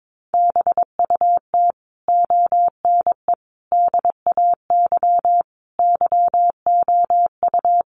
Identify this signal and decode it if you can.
Morse code, heard as a keyed tone, about 22 wpm, 700 Hz. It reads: BUT ONE DAY YOU
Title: Sentences from Top 400 Words 22wpm